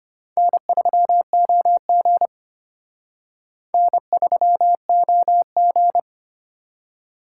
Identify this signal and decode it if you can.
Morse code, heard as a keyed tone, about 31 wpm, 700 Hz. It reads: N3OG N3OG
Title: Speed Racing - USA 1x2 Call Signs 25wpm